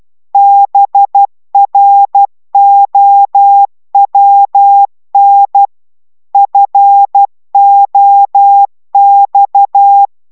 Morse code, heard as a keyed tone, about 12 wpm, 800 Hz. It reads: BROWN FOX
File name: CW 10 baud